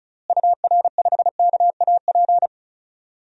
Morse code, heard as a keyed tone, about 34 wpm, 700 Hz. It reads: UR5KAP
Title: DXCC-100 Call Signs - 6 Characters - Repeated 3x - 35wpm